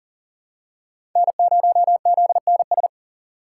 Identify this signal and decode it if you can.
Morse code, heard as a keyed tone, about 40 wpm, 700 Hz. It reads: N0ZNS
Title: USA Call Signs - 6x Character Spacing - 40wpm